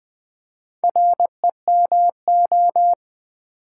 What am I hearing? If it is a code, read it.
Morse code, heard as a keyed tone, about 20 wpm, 700 Hz. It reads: REMO
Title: Build A Word - 2 Letters At A Time - 6-Letter Words - 20wpm